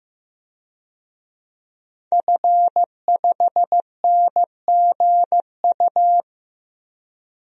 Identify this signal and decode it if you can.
Morse code, heard as a keyed tone, about 15 wpm, 700 Hz. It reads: F5NGU
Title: DXCC-100 Call Signs - 5 Characters - 15wpm